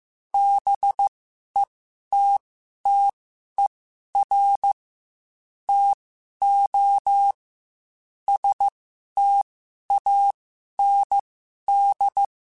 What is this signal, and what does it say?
Morse code, heard as a keyed tone, about 15 wpm, spaced as for 11 wpm, 800 Hz. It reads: BETTER TO STAND
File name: Quotes / Quotes-mp3s / StandFor @13